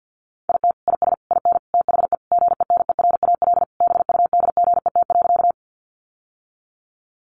Morse code, heard as a keyed tone, about 50 wpm, 700 Hz. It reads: IT IS IN THE GENERAL BUDGET.
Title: Sentences from Top 1200 Words 50wpm